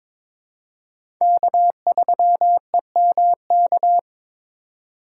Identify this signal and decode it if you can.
Morse code, heard as a keyed tone, about 21 wpm, 700 Hz. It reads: K3EMK
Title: USA Call Signs - 5 Characters - 3x Character Spacing - 22wpm